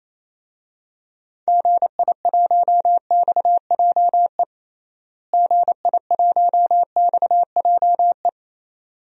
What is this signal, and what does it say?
Morse code, heard as a keyed tone, about 28 wpm, 700 Hz. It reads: GI1XJE GI1XJE
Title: DXCC-100 Call Signs - 6 Characters - Repeated 2x - 28wpm